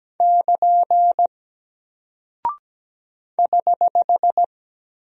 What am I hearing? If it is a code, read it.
Morse code, heard as a keyed tone, about 17 wpm, 700 Hz. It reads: <KN> <HH>
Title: Single Character (Letter:Number:Punctuation:Prosign) Alt 17wpm